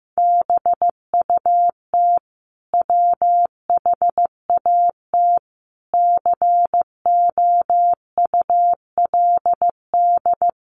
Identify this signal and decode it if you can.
Morse code, heard as a keyed tone, about 15 wpm, 700 Hz. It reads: BUT WHAT COULD
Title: Sentences from Top 1300 Words 15wpm